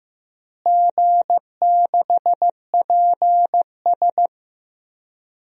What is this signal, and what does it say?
Morse code, heard as a keyed tone, about 15 wpm, 700 Hz. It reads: G6PS
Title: DXCC-100 Call Signs - 4 Characters - Repeated 2x - 15wpm